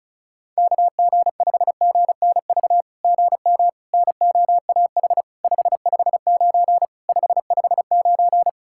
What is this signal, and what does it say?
Morse code, heard as a keyed tone, about 36 wpm, 700 Hz. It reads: KG5GNV GM NOAH 559 559
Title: SOTA QSO Practice 35wpm